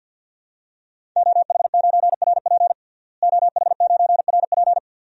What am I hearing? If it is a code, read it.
Morse code, heard as a keyed tone, about 51 wpm, 700 Hz. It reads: OH9RP OH9RP
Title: DXCC-100 Call Signs - 5 Characters - Repeated 2x - 50wpm